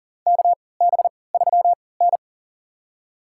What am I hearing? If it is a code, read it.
Morse code, heard as a keyed tone, about 40 wpm, spaced as for 23 wpm, 700 Hz. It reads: KB3N
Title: USA Call Signs - 4 Characters - 3x Character Spacing - 40wpm